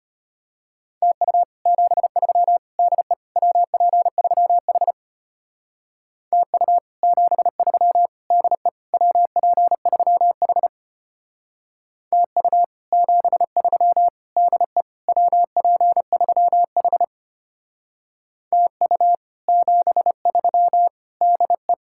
Morse code, heard as a keyed tone, about 29 wpm, 700 Hz. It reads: TU 73 DE WP3H TU 73 DE WP3H TU 73 DE WP3H TU 73 DE
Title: Speed-Racing Common QSO Phrases - 25wpm